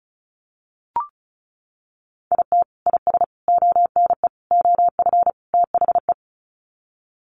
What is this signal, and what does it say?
Morse code, heard as a keyed tone, about 34 wpm, 700 Hz. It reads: IT IS ONE OF THE
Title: Sets of 5 Words 35wpm